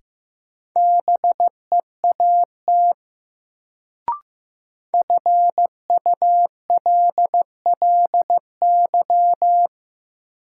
Morse code, heard as a keyed tone, about 15 wpm, 700 Hz. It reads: BEAT FULLY
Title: Sentences from Top 1300 Words - New Word Drill Alt - 15wpm